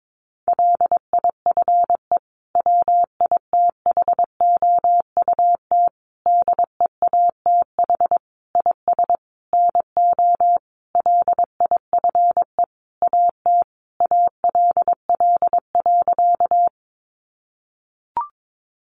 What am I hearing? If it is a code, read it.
Morse code, heard as a keyed tone, about 22 wpm, 700 Hz. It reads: LIFE WITHOUT DEATH IS NO LIFE AT ALL.
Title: Sentences from Top 1100 Words 22wpm